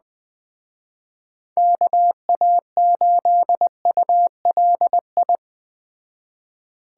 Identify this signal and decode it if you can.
Morse code, heard as a keyed tone, about 21 wpm, 700 Hz. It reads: KA8ULI